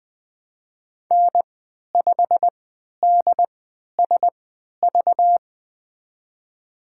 Morse code, heard as a keyed tone, about 20 wpm, spaced as for 11 wpm, 700 Hz. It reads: N5DSV
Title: USA Call Signs - 5 Characters - 3x Character Spacing - 20wpm